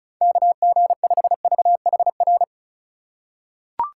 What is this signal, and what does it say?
Morse code, heard as a keyed tone, about 36 wpm, 700 Hz. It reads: KG5VHR
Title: USA Call Signs - 4x Character Spacing - 35wpm